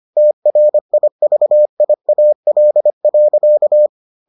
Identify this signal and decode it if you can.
Morse code, heard as a keyed tone, about 25 wpm, 600 Hz. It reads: TRIVIAL.